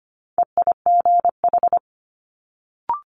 Morse code, heard as a keyed tone, about 25 wpm, 700 Hz. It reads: EIGH
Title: Multi-letter Phonemes Quick Fire 25wpm